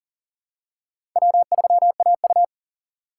Morse code, heard as a keyed tone, about 40 wpm, 700 Hz. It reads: W3AU
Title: USA Call Signs - 4 Characters - 6x Character Spacing - 40wpm